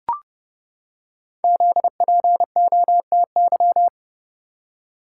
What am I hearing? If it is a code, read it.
Morse code, heard as a keyed tone, about 29 wpm, 700 Hz. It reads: ZPOTY
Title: Code Groups - 5 Letters - 30wpm